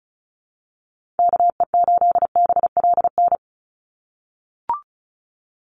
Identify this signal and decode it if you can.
Morse code, heard as a keyed tone, about 34 wpm, 700 Hz. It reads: KE8BLN
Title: USA Call Signs 35wpm